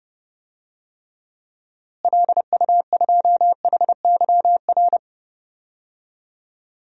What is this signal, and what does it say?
Morse code, heard as a keyed tone, about 32 wpm, 700 Hz. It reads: LU2HYR
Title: DXCC-100 Call Signs - 6 Characters - Repeated 3x - 30wpm